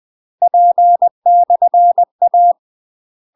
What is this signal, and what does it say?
Morse code, heard as a keyed tone, about 20 wpm, 700 Hz. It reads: P/A
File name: UK MASTER QSOs TRAINING / 20WPM / UK MASTER QSOs 0032